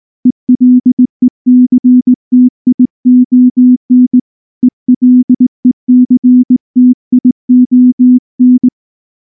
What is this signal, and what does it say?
Morse code, heard as a keyed tone, about 18 wpm, 250 Hz. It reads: ELECTION ELECTION
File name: morse / morse-training-20WPM-1